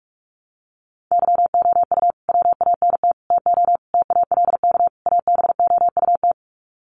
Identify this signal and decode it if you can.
Morse code, heard as a keyed tone, about 44 wpm, 700 Hz. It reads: YOU WANT TO TALK ABOUT
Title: Sets of 5 Words Encore 45wpm